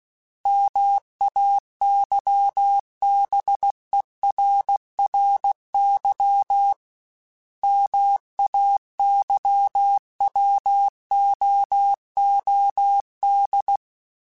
Morse code, heard as a keyed tone, about 16 wpm, 800 Hz. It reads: MAYBERRY MAYWOOD
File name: fromSMs / FromSMs mp3s / Cities@15